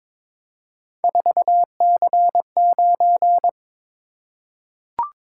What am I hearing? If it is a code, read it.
Morse code, heard as a keyed tone, about 23 wpm, 700 Hz. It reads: 4C9
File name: practice / 22wpm / Three Letter-Number Combinations - 22wpm